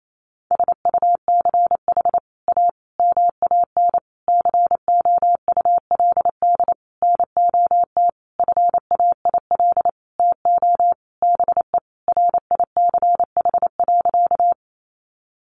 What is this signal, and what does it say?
Morse code, heard as a keyed tone, about 28 wpm, 700 Hz. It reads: SUCH A MAN COULD NOT FAIL TO BE RICH.
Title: Sentences from Top 1100 Words 28wpm